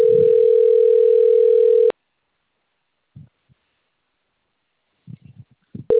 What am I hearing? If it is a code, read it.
Ringback tone (North America), 440 + 480 Hz.
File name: rg-1600-unknown- - - .37544